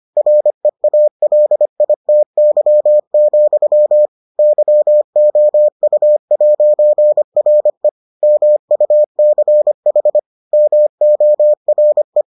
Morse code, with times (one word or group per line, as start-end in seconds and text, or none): 0.2-4.1 REALITY,
4.4-7.9 YOU'RE
8.2-10.2 MUCH
10.5-12.2 MORE